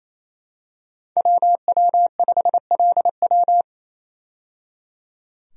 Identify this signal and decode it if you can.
Morse code, heard as a keyed tone, about 28 wpm, 700 Hz. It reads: WW5LW
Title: USA Call Signs - 5 Characters - 28wpm